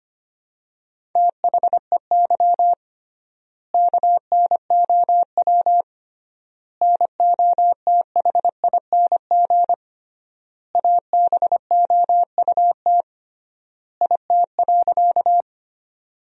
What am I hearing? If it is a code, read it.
Morse code, heard as a keyed tone, about 25 wpm, 700 Hz. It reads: THEY KNOW NOTHING ABOUT IT.
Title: 3x Word-Spacing - Sentences from Top 500 Words - 25wpm